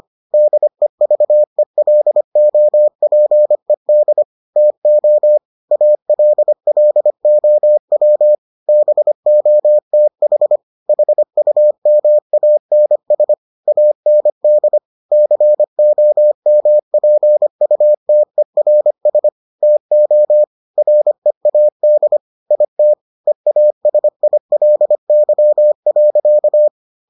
Morse code, heard as a keyed tone, about 25 wpm, 600 Hz. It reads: DEVELOPED TO ALLOW BOTH HUMANS AND COMPUTERS TO READ IT EASILY.